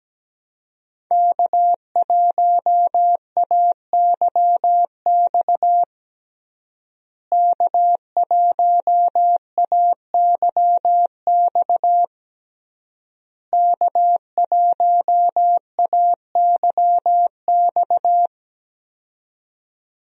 Morse code, heard as a keyed tone, about 17 wpm, 700 Hz. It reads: K1AYX K1AYX K1AYX
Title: USA Call Signs - 5 Characters - Repeated 3x - 17wpm